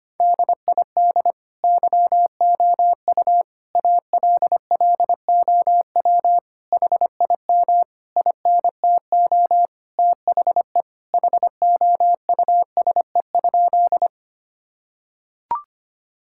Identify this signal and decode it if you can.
Morse code, heard as a keyed tone, about 25 wpm, 700 Hz. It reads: DID YOU ALLOW HIM INTO THE HOUSE?
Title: Sentences from Top 1000 Words - Review Part 5 - 25wpm